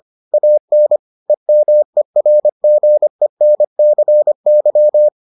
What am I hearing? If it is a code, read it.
Morse code, heard as a keyed tone, about 25 wpm, 600 Hz. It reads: AN EMERGENCY